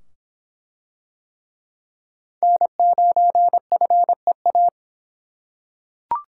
Morse code, heard as a keyed tone, about 24 wpm, 700 Hz. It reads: N9FEA